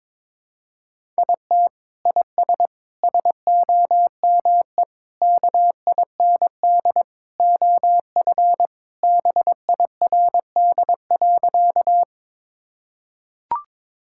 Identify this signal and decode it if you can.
Morse code, heard as a keyed tone, about 22 wpm, 700 Hz. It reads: IT IS SOME KIND OF BIRD.
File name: practice / 22wpm / Sentences with 3-Word Phrases - 22wpm